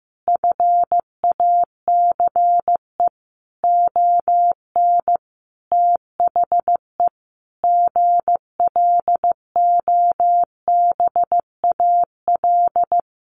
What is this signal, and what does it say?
Morse code, heard as a keyed tone, about 15 wpm, 700 Hz. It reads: FACE ON THE GLOBAL
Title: Sentences from Top 1200 Words 15wpm